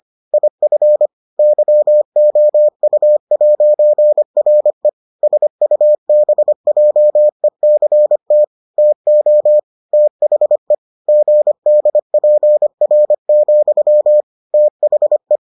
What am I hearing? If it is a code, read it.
Morse code, heard as a keyed tone, about 25 wpm, 600 Hz. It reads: IF YOU'RE SUBJECT TO THE GDPR, THE